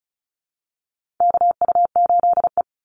Morse code, heard as a keyed tone, about 34 wpm, 700 Hz. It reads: KU8E
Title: Call Signs Repeated 3x - 35wpm